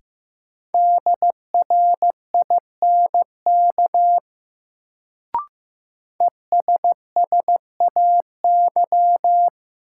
Morse code, heard as a keyed tone, about 15 wpm, 700 Hz. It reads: DRINK ESSAY
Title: Sentences from Top 1500 Words - New Word Drill - 15wpm